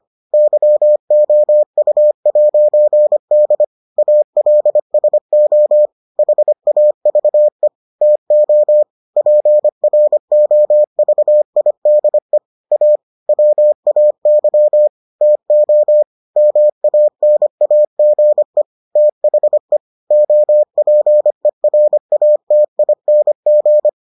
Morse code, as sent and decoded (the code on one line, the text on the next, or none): -.-- --- ..- .----. -.. / .- .-.. ... --- / .... .- ...- . / - --- / .--. .-. --- ...- .. -.. / .- / .-- .- -.-- / - --- / -- .- -. .- --. / - .... / --- .--. .-. .- - .. -. --.
YOU'D ALSO HAVE TO PROVIDE A WAY TO MANAGE THE OPERATING